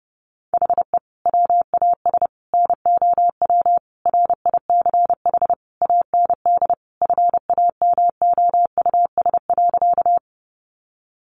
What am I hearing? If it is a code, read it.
Morse code, heard as a keyed tone, about 30 wpm, 700 Hz. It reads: HE WAS NOW RICH AND FAMOUS.
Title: Sentences from Top 1500 Words 30wpm